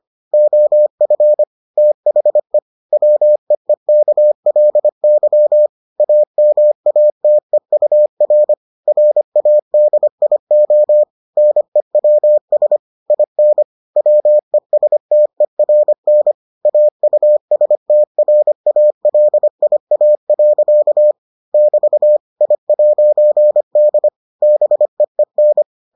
Morse code, heard as a keyed tone, about 25 wpm, 600 Hz. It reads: OF THE WEEKLY AMATEUR RADIO NEWS IN WESTERN AUSTRALIA. <BT> I'D BEEN